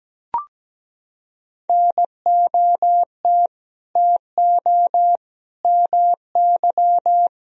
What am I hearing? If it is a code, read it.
Morse code, heard as a keyed tone, about 17 wpm, 700 Hz. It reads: NOT TO MY
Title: Sentences from Top 1000 Words - Review Part 4 - 17wpm